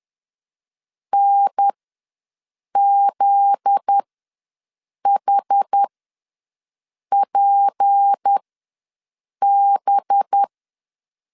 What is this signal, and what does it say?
Morse code, heard as a keyed tone, about 10 wpm, spaced as for 6 wpm, 800 Hz. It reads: NZHPB
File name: LEZIONI / 7 GF9Z / 7 GF9Z ws 10wpm 3919